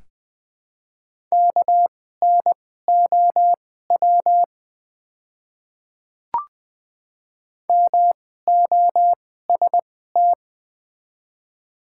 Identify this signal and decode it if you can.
Morse code, heard as a keyed tone, about 20 wpm, spaced as for 15 wpm, 700 Hz. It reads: KNOW MOST
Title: 2x Character Spacing - Top 100 Words - 20wpm